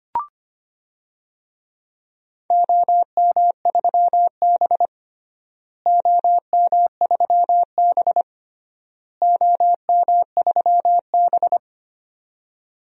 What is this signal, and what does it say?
Morse code, heard as a keyed tone, about 25 wpm, 700 Hz. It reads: OM3B OM3B OM3B
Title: DXCC-100 Call Signs - 4 Characters - Repeated 3x - 25wpm